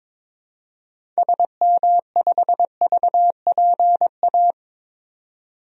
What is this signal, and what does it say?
Morse code, heard as a keyed tone, about 22 wpm, 700 Hz. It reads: SM5VPA
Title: DXCC-100 Call Signs - 6 Characters - 22wpm